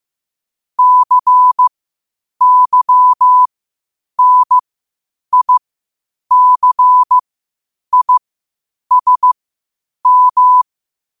Morse code, heard as a keyed tone, about 15 wpm, spaced as for 8 wpm, 1000 Hz. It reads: CYNICISM
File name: qotdfiles / 8wpm